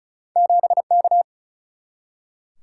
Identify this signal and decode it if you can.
Morse code, heard as a keyed tone, about 36 wpm, 700 Hz. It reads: 7K
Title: Single-Letter Single-Number 35wpm